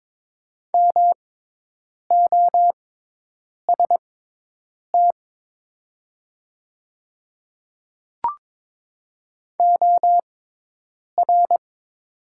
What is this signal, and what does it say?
Morse code, heard as a keyed tone, about 22 wpm, spaced as for 8 wpm, 690 Hz. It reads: MOST OR
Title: 6x Character Spacing - Top 200 Words - 22wpm